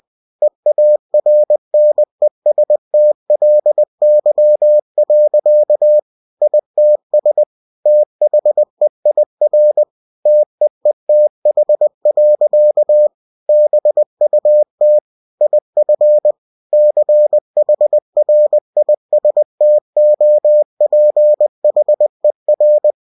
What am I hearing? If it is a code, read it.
Morse code, heard as a keyed tone, about 20 wpm, 600 Hz. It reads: EARNESTLY. ITS THEIR TEETH. BUT IF CHRISTOPHER
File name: cw-books / winnie-the-pooh / winnie-20wpm-page-0079